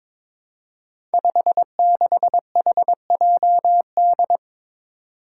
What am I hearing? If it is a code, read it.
Morse code, heard as a keyed tone, about 22 wpm, 700 Hz. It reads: 56HJD
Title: Five Challenging Letter-Number Combinations 22wpm